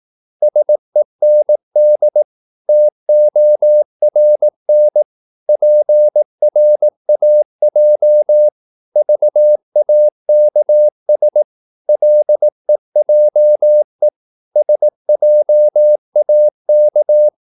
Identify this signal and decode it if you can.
Morse code, heard as a keyed tone, about 18 wpm, 600 Hz. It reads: SEND TORN PRAJ VAKS LEJE SJAK
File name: Korte ord DK 18wpm